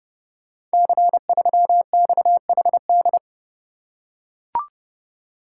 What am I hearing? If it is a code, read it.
Morse code, heard as a keyed tone, about 30 wpm, 700 Hz. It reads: C3XHD
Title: Code Groups - 5 Letter:Numbers - 30wpm